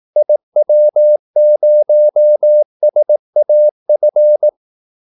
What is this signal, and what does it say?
Morse code, heard as a keyed tone, about 18 wpm, 600 Hz. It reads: IW0SAF
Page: Kaldesignaler 18 wpm